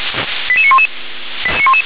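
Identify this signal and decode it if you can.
Five-tone selective call (ZVEI), heard as quick runs of five tones, tones of about 70 ms. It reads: 88911
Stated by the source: ZVEI five-tone selective call received on SDR.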